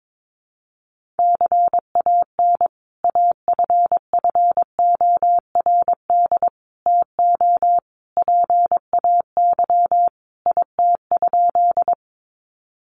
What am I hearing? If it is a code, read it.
Morse code, heard as a keyed tone, about 22 wpm, 700 Hz. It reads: CAN AFFORD TO PAY IT?
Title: Sentences from Top 1100 Words 22wpm